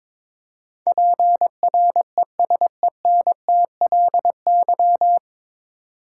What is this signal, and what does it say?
Morse code, heard as a keyed tone, about 22 wpm, 700 Hz. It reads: PRESENTLY
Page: Morse Code Ninja practice 22wpm